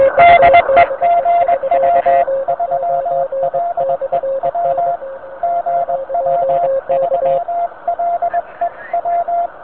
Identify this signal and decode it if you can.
Morse code, heard as a keyed tone, about 20 wpm, 690 Hz, with noise. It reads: DE GW3LEW GW3LEW
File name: VP9-N0JK-pile